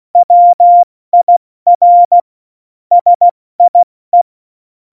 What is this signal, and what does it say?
Morse code, heard as a keyed tone, about 16 wpm, 700 Hz. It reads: WIR SIE
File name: Morsekurs / Witze / Witz 22. August 17, Tempo 70 F80